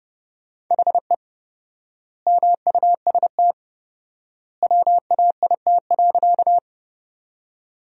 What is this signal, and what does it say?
Morse code, heard as a keyed tone, about 32 wpm, 700 Hz. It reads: HE MUST WAIT.
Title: 4x Word-Spacing - Sentences from Top 500 Words - 30wpm